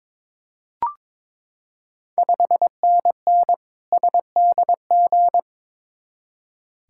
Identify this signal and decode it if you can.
Morse code, heard as a keyed tone, about 22 wpm, 700 Hz. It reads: 5NN SDG
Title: Ontario QSO Party Exchange 22wpm